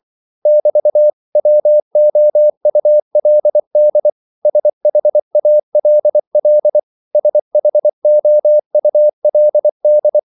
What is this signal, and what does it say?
Morse code, heard as a keyed tone, about 24 wpm, 600 Hz. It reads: <BT>WOULD SHALL SHOULD
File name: CW / Practice / 500MostCommonEnglishWords24